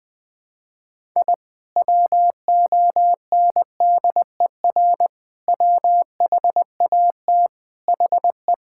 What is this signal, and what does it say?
Morse code, heard as a keyed tone, about 21 wpm, 700 Hz. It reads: I WONDER WHAT HE
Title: Sentences with 2-Word Phrases - 20wpm